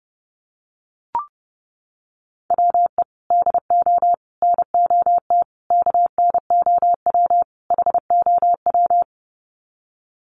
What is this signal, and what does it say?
Morse code, heard as a keyed tone, about 30 wpm, 700 Hz. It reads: WE DO NOT KNOW HOW
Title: Sets of 5 Words Encore 30wpm